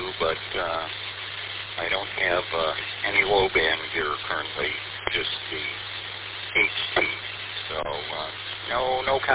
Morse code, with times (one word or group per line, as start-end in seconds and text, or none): none